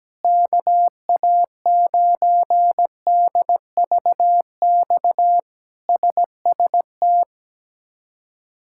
Morse code, heard as a keyed tone, about 17 wpm, 700 Hz. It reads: KA9DVX SST
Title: K1USN SST QSO Practice - 17wpm